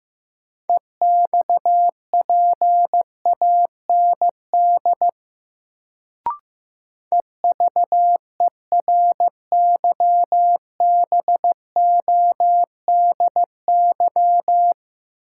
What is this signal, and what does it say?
Morse code, heard as a keyed tone, about 15 wpm, 700 Hz. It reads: EXPAND EVERYBODY
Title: Sentences from Top 1400 Words - New Word Drill Alt - 15wpm